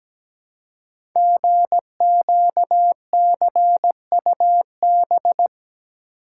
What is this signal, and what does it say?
Morse code, heard as a keyed tone, about 17 wpm, 700 Hz. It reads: GQCUB
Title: Code Groups - 5 Letter:Numbers - 17wpm